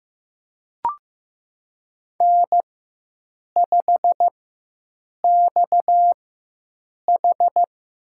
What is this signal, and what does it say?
Morse code, heard as a keyed tone, about 15 wpm, spaced as for 7 wpm, 700 Hz. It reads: N5XH
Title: USA Call Signs - 4x Character Spacing - 15wpm